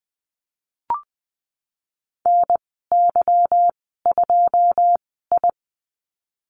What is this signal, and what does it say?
Morse code, heard as a keyed tone, about 19 wpm, spaced as for 14 wpm, 700 Hz. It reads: NY2I
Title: 154 Lesson 25g - Letter p - Call Signs 20wpm